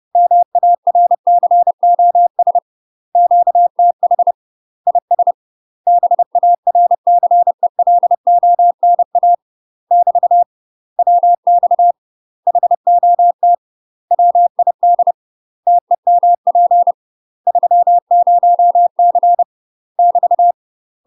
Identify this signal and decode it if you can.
Morse code, heard as a keyed tone, about 30 wpm, 700 Hz. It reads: MARCOS QTH IS BARCELONA <BT> WX HOT WID TEMP 30C <BT>
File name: UK MASTER QSOs TRAINING / 30WPM / UK MASTER QSOs 0004